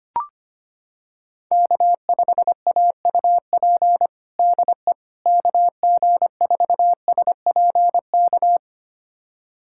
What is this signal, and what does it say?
Morse code, heard as a keyed tone, about 25 wpm, 700 Hz. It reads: K5AUP DE KG4SPK
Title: USA Callsign DE Callsign 25wpm